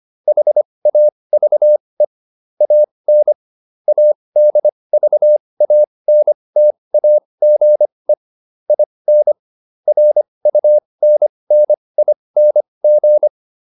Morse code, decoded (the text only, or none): HAVE AN ADVANTAGE IN RUNNING